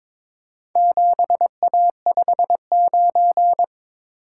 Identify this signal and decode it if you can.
Morse code, heard as a keyed tone, about 23 wpm, 700 Hz. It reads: 7A59
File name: practice / 22wpm / Four Letter-Number Combinations - 22wpm